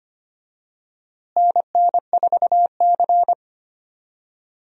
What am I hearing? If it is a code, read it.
Morse code, heard as a keyed tone, about 25 wpm, 700 Hz. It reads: NN4C
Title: USA Call Signs - 4 Characters - Repeated 3x - 25wpm